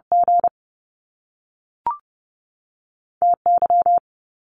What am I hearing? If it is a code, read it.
Morse code, heard as a keyed tone, about 30 wpm, 700 Hz. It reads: G T Y
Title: Top Two Letter Combinations in Words 30wpm